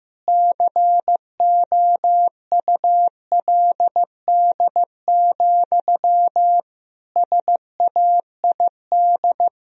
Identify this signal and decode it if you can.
Morse code, heard as a keyed tone, about 15 wpm, 700 Hz. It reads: COULD, SAID
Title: Sentences from Top 1000 Words - Review Part 4 - 15wpm